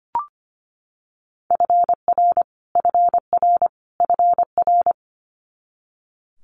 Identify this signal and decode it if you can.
Morse code, heard as a keyed tone, about 25 wpm, 700 Hz. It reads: FR FR FR
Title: Multi-letter Phonemes Repeated 3x 25wpm